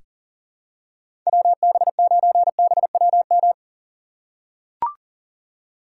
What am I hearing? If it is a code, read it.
Morse code, heard as a keyed tone, about 40 wpm, 700 Hz. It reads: WB9BWM
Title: USA Call Signs - 4x Character Spacing - 40wpm